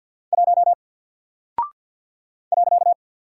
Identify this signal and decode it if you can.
Morse code, heard as a keyed tone, about 49 wpm, 700 Hz. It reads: 1 .
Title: Single Character (Letter:Number:Punctuation:Prosign) Alt 50wpm